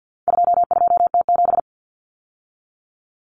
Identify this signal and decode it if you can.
Morse code, heard as a keyed tone, about 48 wpm, 700 Hz. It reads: 22T7
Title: Common Cut Numbers - 4-Digit 50wpm